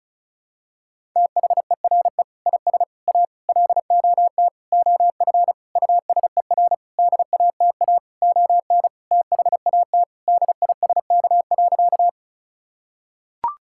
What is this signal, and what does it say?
Morse code, heard as a keyed tone, about 34 wpm, 700 Hz. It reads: THERE IS A LOT OF USER DATA ON THAT DISK.
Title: Sentences from Top 1000 Words - Review Part 6 - 35wpm